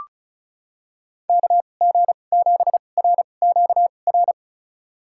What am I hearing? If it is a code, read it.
Morse code, heard as a keyed tone, about 36 wpm, spaced as for 25 wpm, 700 Hz. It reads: KG7RQR
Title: USA Call Signs - 6 Characters - 2x Character Spacing - 35wpm